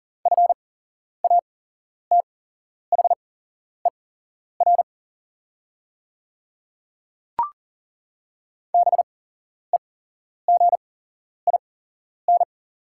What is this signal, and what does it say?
Morse code, heard as a keyed tone, about 40 wpm, spaced as for 11 wpm, 690 Hz. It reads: FATHER BEGIN
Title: 8x Character Spacing - Top 500 Words - 40wpm